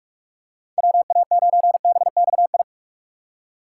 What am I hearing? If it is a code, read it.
Morse code, heard as a keyed tone, about 44 wpm, 700 Hz. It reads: WA9BXI
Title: USA Call Signs - 2x Character Spacing - 45wpm